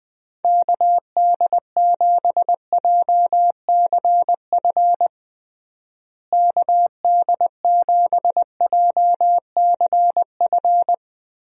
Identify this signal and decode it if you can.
Morse code, heard as a keyed tone, about 20 wpm, 700 Hz. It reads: KD7JCF KD7JCF
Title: USA Call Signs - 6 Characters - Repeated 2x - 20wpm